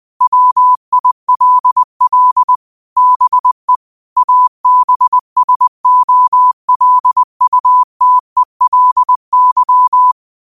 Morse code, heard as a keyed tone, about 20 wpm, 1000 Hz. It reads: WILL BE ABSOLUTELY
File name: qotdfiles / 20wpm